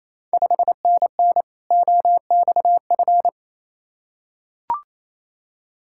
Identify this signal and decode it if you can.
Morse code, heard as a keyed tone, about 28 wpm, 700 Hz. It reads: 5NN OXF
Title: Ontario QSO Party Exchange 28wpm